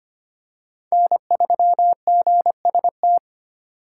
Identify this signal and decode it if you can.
Morse code, heard as a keyed tone, about 25 wpm, 700 Hz. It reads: N3GST
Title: USA Call Signs - 5 Characters - 2x Character Spacing - 25wpm